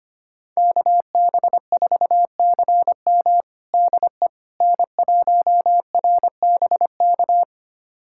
Morse code, heard as a keyed tone, about 25 wpm, 700 Hz. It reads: KB4CM DE N1RBK
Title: USA Callsign DE Callsign - 5 Characters - 25wpm